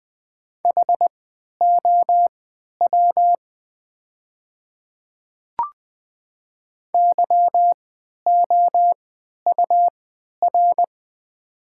Morse code, heard as a keyed tone, about 20 wpm, spaced as for 11 wpm, 700 Hz. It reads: HOW YOUR
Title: 3x Character Spacing - Top 100 Words - 20wpm